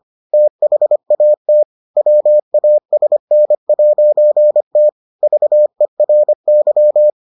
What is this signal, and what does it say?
Morse code, heard as a keyed tone, about 25 wpm, 600 Hz. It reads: THAT WASN'T VERY